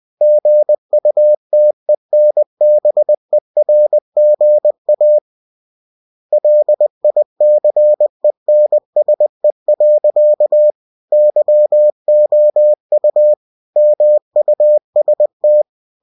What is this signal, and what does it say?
Morse code, heard as a keyed tone, about 20 wpm, 600 Hz. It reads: GUTENBERGA LICENSE. YOU MUST
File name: cw-books / winnie-the-pooh / winnie-20wpm-page-0117